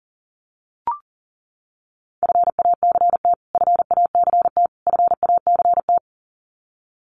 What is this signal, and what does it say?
Morse code, heard as a keyed tone, about 40 wpm, 700 Hz. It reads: FACT FACT FACT
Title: Top 500 Words Repeated 40wpm